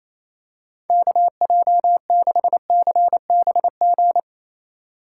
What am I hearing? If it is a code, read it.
Morse code, heard as a keyed tone, about 28 wpm, 700 Hz. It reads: KJ6CBG
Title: USA Call Signs - 2x Character Spacing - 28wpm